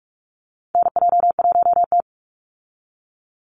Morse code, heard as a keyed tone, about 44 wpm, 700 Hz. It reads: NJ1T
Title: Call Signs Repeated 3x - 45wpm